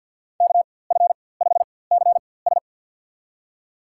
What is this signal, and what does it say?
Morse code, heard as a keyed tone, about 50 wpm, spaced as for 23 wpm, 700 Hz. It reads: KF5CS
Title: USA Call Signs - 5 Characters - 4x Character Spacing - 50wpm